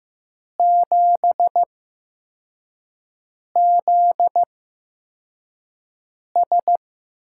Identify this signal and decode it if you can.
Morse code, heard as a keyed tone, about 15 wpm, spaced as for 4 wpm, 700 Hz. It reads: 7ZS
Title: USA Call Signs - 5 Characters - 8x Character Spacing - 15wpm